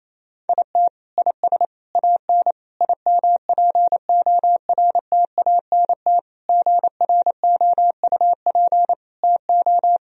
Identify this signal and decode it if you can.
Morse code, heard as a keyed tone, about 28 wpm, 700 Hz. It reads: IT IS AN IMPORTANT GROUP TO
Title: Sentences from Top 1000 Words - Review Part 1 - 28wpm